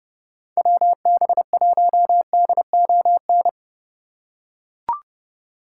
Morse code, heard as a keyed tone, about 29 wpm, 700 Hz. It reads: WB1DON